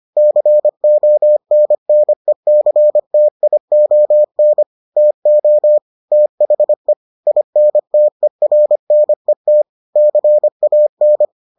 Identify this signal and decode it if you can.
Morse code, heard as a keyed tone, about 25 wpm, 600 Hz. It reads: CONNECTION TO THE INTERNET CAN